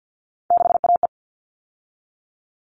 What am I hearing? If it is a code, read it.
Morse code, heard as a keyed tone, about 50 wpm, 700 Hz. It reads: 6AE
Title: All Cut Numbers - 3-Digit 50wpm